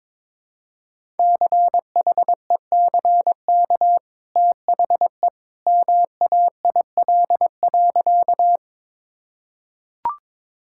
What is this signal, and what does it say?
Morse code, heard as a keyed tone, about 22 wpm, 700 Hz. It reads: CHECK THE MAIL.
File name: practice / 22wpm / Easy Sentences with Hints - 22wpm